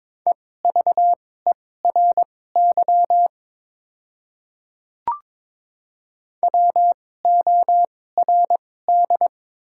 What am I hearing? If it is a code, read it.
Morse code, heard as a keyed tone, about 23 wpm, spaced as for 16 wpm, 700 Hz. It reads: EVERY WORD
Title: 2x Character Spacing - Top 300 Words - 22wpm